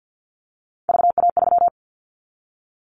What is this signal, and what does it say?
Morse code, heard as a keyed tone, about 49 wpm, 700 Hz. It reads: VA3
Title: Callsign Prefixes for Top 100 DXCC - 50wpm